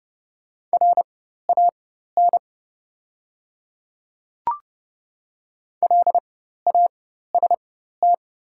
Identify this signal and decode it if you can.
Morse code, heard as a keyed tone, about 29 wpm, spaced as for 14 wpm, 700 Hz. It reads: RAN LAST
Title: 4x Character Spacing - Top 500 Words - 30wpm